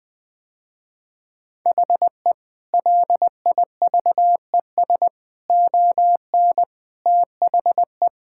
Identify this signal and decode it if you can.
Morse code, heard as a keyed tone, about 21 wpm, 700 Hz. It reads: HE LIVES ON THE
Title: Sentences with 6-Word Phrases - Spoken Beginning - 20wpm